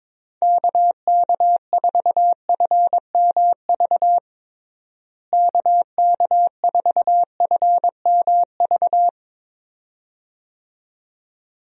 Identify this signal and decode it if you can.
Morse code, heard as a keyed tone, about 22 wpm, 700 Hz. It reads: KK4FMV KK4FMV
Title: USA Call Signs - 6 Characters - Repeated 2x - 22wpm